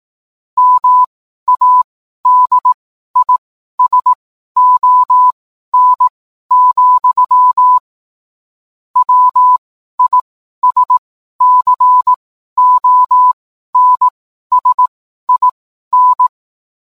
Morse code, heard as a keyed tone, about 18 wpm, spaced as for 12 wpm, 1000 Hz. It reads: MADISON, WISCONSIN